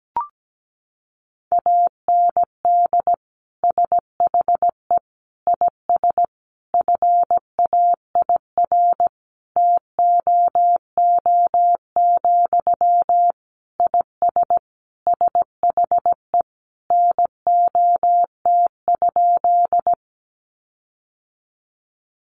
Morse code, heard as a keyed tone, about 17 wpm, 700 Hz. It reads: AND SHE IS FAIR TOO, IS SHE NOT?
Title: Sentences from Top 1300 Words 17wpm